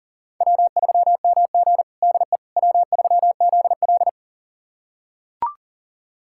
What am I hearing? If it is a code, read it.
Morse code, heard as a keyed tone, about 40 wpm, 700 Hz. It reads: W3MG DE W3ZL
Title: USA Callsign DE Callsign - 4 Characters - 40wpm